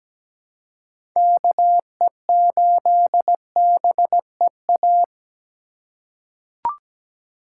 Morse code, heard as a keyed tone, about 17 wpm, 700 Hz. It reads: KE8BEA
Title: USA Call Signs - 6 Characters - 3x Character Spacing - 17wpm